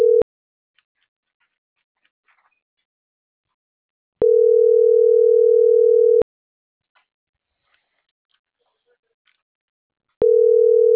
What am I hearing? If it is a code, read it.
Ringback tone (North America), 440 + 480 Hz.